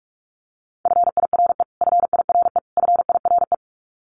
Morse code, heard as a keyed tone, about 44 wpm, 700 Hz. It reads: FIRE FIRE FIRE
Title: Top 500 Words Repeated 45wpm